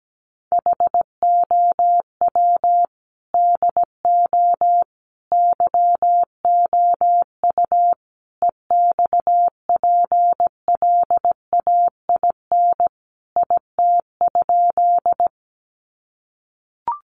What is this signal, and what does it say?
Morse code, heard as a keyed tone, about 17 wpm, 700 Hz. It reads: HOW DO YOU EXPLAIN IT?
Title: Sentences from Top 1100 Words 17wpm